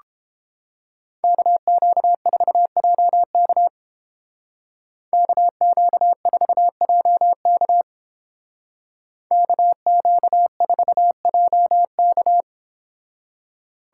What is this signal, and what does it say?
Morse code, heard as a keyed tone, about 30 wpm, 700 Hz. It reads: KQ4JK KQ4JK KQ4JK